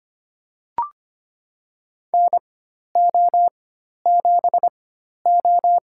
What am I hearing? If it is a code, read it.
Morse code, heard as a keyed tone, about 25 wpm, spaced as for 12 wpm, 700 Hz. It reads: NO7O
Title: USA Call Signs - 4x Character Spacing - 25wpm